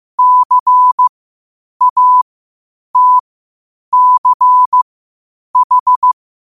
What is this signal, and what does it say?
Morse code, heard as a keyed tone, about 15 wpm, spaced as for 8 wpm, 1000 Hz. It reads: CATCH
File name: qotdfiles / 8wpm